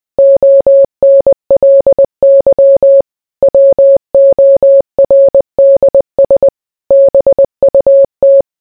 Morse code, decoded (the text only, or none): ONLY WORDS BUT